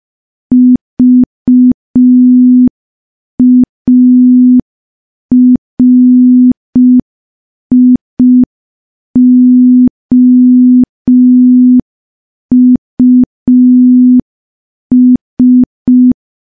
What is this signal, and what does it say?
Morse code, heard as a keyed tone, about 5 wpm, 250 Hz. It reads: VARIOUS
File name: morse / morse-training-05WPM